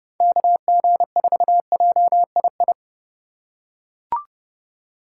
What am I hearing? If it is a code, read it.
Morse code, heard as a keyed tone, about 30 wpm, 700 Hz. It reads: KG4JII
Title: USA Call Signs - 6 Characters - 3x Character Spacing - 30wpm